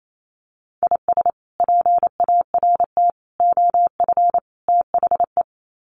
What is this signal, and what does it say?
Morse code, heard as a keyed tone, about 28 wpm, 700 Hz. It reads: IS PART OF THE
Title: Sets of 4 Words 28wpm